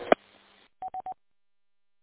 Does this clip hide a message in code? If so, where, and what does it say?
No.